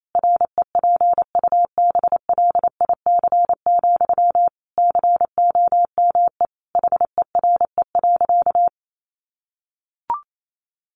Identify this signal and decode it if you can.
Morse code, heard as a keyed tone, about 28 wpm, 700 Hz. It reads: REPUBLIC, COME HERE.
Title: Sentences from Top 1500 Words 28wpm